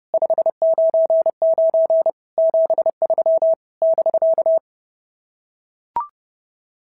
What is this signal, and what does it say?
Morse code, heard as a keyed tone, about 30 wpm, 650 Hz. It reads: 599 73 <BK>
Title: SOTA QSO Practice 30wpm